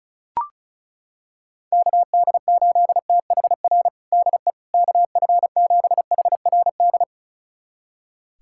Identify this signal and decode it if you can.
Morse code, heard as a keyed tone, about 34 wpm, 700 Hz. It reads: KD8THR DE KF7HRD
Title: USA Callsign DE Callsign - 6 Characters - 35wpm